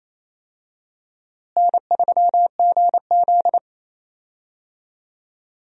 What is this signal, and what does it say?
Morse code, heard as a keyed tone, about 27 wpm, 700 Hz. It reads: N3GZ
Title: USA Call Signs - 4 Characters - Repeated 3x - 28wpm